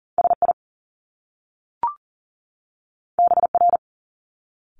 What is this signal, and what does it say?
Morse code, heard as a keyed tone, about 38 wpm, 700 Hz. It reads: SI BR
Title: Top Two Letter Combinations in Words 40wpm